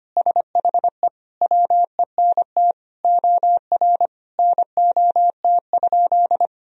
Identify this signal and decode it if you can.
Morse code, heard as a keyed tone, about 25 wpm, 700 Hz. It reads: SHE WENT OR NOT?
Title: Sentences from Top 1000 Words - Review Part 3 - 25wpm